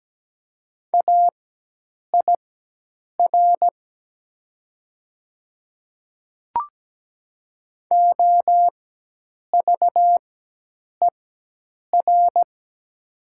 Morse code, heard as a keyed tone, about 17 wpm, spaced as for 8 wpm, 700 Hz. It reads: AIR OVER
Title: 4x Character Spacing - Top 300 Words - 17wpm